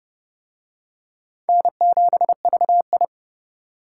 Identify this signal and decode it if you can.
Morse code, heard as a keyed tone, about 30 wpm, 700 Hz. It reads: N7VI